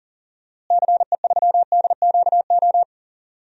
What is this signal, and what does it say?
Morse code, heard as a keyed tone, about 40 wpm, 700 Hz. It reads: CE3DQO
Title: DXCC-100 Call Signs - 6 Characters - Repeated 2x - 40wpm